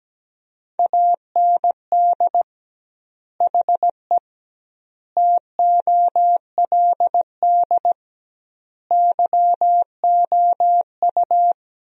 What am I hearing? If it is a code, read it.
Morse code, heard as a keyed tone, about 17 wpm, 700 Hz. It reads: AND HE TOLD YOU
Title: 2x Word-Spacing - Sentences from Top 500 Words - 17wpm